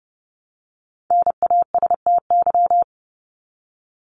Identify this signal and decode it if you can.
Morse code, heard as a keyed tone, about 30 wpm, 700 Hz. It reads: NASTY
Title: Top 1751-2000 Words 30wpm